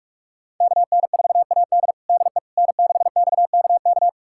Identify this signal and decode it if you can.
Morse code, heard as a keyed tone, about 46 wpm, 700 Hz. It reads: KN4AD DE N6XKK
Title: USA Callsign DE Callsign - 5 Characters - 45wpm